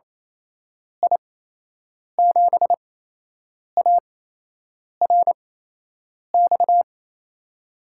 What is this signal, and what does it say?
Morse code, heard as a keyed tone, about 28 wpm, spaced as for 8 wpm, 700 Hz. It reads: I7ARX
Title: USA Call Signs - 8x Character Spacing - 28wpm